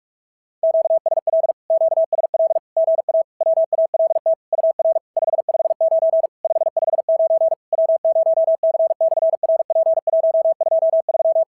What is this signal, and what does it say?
Morse code, heard as a keyed tone, about 44 wpm, 650 Hz. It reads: QSL QSL GA WALT UR 559 559 W0C/RP123
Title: SOTA QSO Practice 45wpm